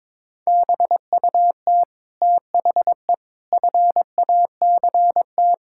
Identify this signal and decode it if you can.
Morse code, heard as a keyed tone, about 22 wpm, 700 Hz. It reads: BUT THE FACT
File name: Sentences with 9-Word Phrases - 22wpm